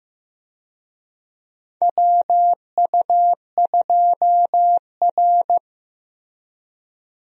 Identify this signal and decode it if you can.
Morse code, heard as a keyed tone, about 15 wpm, 700 Hz. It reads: WU2R
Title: USA Call Signs - 4 Characters - Repeated 2x - 15wpm